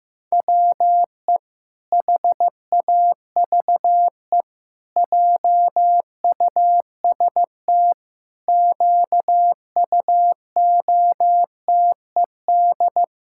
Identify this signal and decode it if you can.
Morse code, heard as a keyed tone, about 15 wpm, 700 Hz. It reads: WE HAVE JUST QUOTED
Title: Pangram Challenge v2 - 15wpm